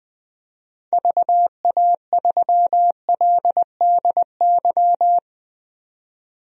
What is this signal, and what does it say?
Morse code, heard as a keyed tone, about 21 wpm, 700 Hz. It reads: VA3LDY
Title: DXCC-100 Call Signs - 6 Characters - 20wpm